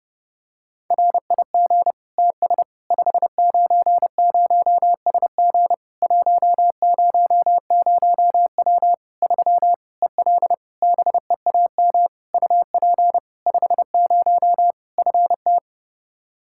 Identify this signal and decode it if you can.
Morse code, heard as a keyed tone, about 32 wpm, 700 Hz. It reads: RIG TS 590SG 100W 3 EL BEAM UP 50 FT
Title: K1USN SST QSO Practice 30wpm